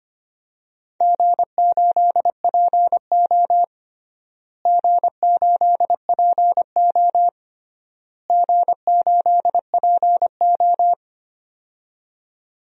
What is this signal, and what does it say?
Morse code, heard as a keyed tone, about 25 wpm, 700 Hz. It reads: G8PO G8PO G8PO
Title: DXCC-100 Call Signs - 4 Characters - Repeated 3x - 25wpm